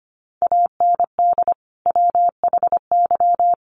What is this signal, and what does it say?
Morse code, heard as a keyed tone, about 25 wpm, 700 Hz. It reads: AND WHY
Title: Sentences from Top 300 Words 25wpm